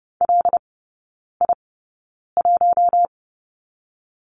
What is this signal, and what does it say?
Morse code, heard as a keyed tone, about 30 wpm, spaced as for 9 wpm, 700 Hz. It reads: LI1
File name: Lesson 15e - Number 5 - Three Characters 30wpm